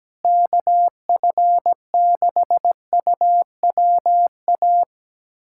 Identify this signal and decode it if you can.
Morse code, heard as a keyed tone, about 17 wpm, 700 Hz. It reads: KF6UWA
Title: USA Callsign DE Callsign 17wpm